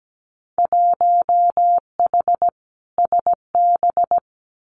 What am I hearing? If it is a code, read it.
Morse code, heard as a keyed tone, about 17 wpm, 700 Hz. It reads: 1H SB
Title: Winter Field Day Exchanges 17wpm